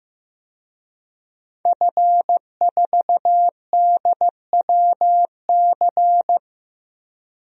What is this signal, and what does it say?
Morse code, heard as a keyed tone, about 15 wpm, 700 Hz. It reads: F4DWC
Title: DXCC-100 Call Signs - 5 Characters - Repeated 2x - 15wpm